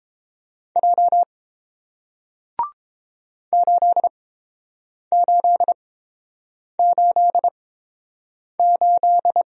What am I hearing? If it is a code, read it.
Morse code, heard as a keyed tone, about 30 wpm, spaced as for 7 wpm, 700 Hz. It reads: J 8888